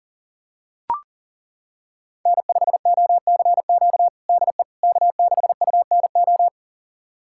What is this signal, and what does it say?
Morse code, heard as a keyed tone, about 40 wpm, 700 Hz. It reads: N5OCQ DE K6UNO
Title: USA Callsign DE Callsign - 5 Characters - 40wpm